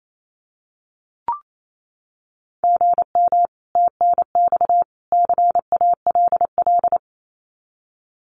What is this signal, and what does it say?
Morse code, heard as a keyed tone, about 28 wpm, 700 Hz. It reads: GM TNX CALL
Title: Common QSO Phrases 28wpm